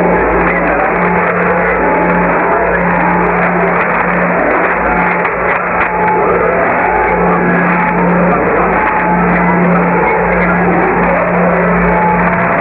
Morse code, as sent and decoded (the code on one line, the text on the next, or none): none